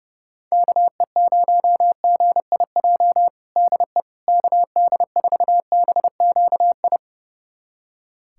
Morse code, heard as a keyed tone, about 32 wpm, 700 Hz. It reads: KE0GIJ DE KD4BQI
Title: USA Callsign DE Callsign - 6 Characters - 30wpm